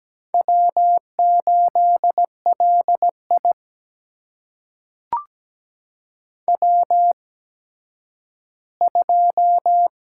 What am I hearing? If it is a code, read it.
Morse code, heard as a keyed tone, about 17 wpm, spaced as for 10 wpm, 700 Hz. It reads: W8LI W2
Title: USA Call Signs - 4 Characters - 8x Character Spacing - 17wpm